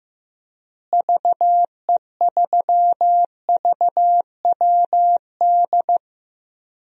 Morse code, heard as a keyed tone, about 15 wpm, 700 Hz. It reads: VE3VWD
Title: DXCC-100 Call Signs - 6 Characters - Repeated 3x - 15wpm